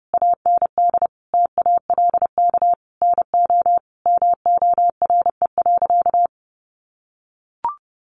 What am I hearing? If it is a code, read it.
Morse code, heard as a keyed tone, about 30 wpm, 700 Hz. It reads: AND TALK NO MORE.
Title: Sentences from Top 1400 Words 30wpm